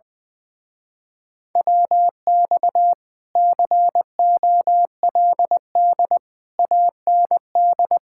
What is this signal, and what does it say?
Morse code, heard as a keyed tone, about 20 wpm, 700 Hz. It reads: WX COLD AND
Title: Speed-Racing Common QSO Phrases - 20wpm